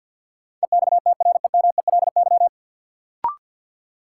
Morse code, heard as a keyed tone, about 51 wpm, 700 Hz. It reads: EXTREMELY
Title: Sentences from Top 1000 Words - Review Part 7 - New Word Drill - 50wpm